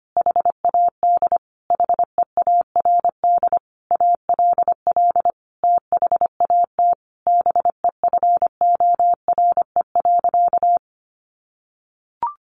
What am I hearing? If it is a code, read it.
Morse code, heard as a keyed tone, about 25 wpm, 700 Hz. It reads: HAD HEARD ALL THAT BEFORE.
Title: Sentences from Top 400 Words 25wpm